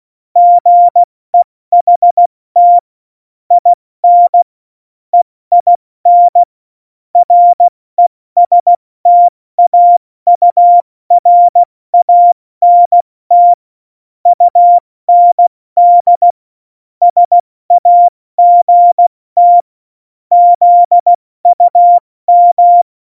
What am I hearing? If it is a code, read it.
Morse code, heard as a keyed tone, about 16 wpm, spaced as for 14 wpm, 700 Hz. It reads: GEHT IN EIN RESTAURANT UND SAGT ZUM
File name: Morsekurs / Witze / Witz 23. April 2019, 70 F80